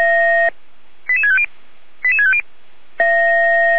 Five-tone selective call (ZVEI), heard as quick runs of five tones, tones of about 70 ms. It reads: 88650 88650
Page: SDR/ZVEI